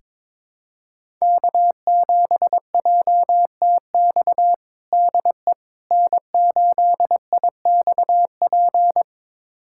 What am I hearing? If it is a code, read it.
Morse code, heard as a keyed tone, about 21 wpm, 700 Hz. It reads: K7JTX DE N8IXP
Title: USA Callsign DE Callsign - 5 Characters - 22wpm